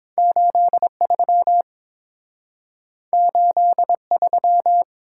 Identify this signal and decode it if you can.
Morse code, heard as a keyed tone, about 30 wpm, 700 Hz. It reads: 83 83